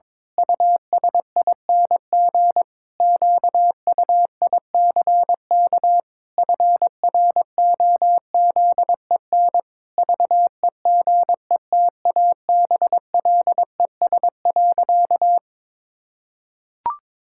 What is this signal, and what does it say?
Morse code, heard as a keyed tone, about 22 wpm, 700 Hz. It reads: USING QUICK FROZEN VEGETABLES.
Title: Pangram Challenge v2 - 22wpm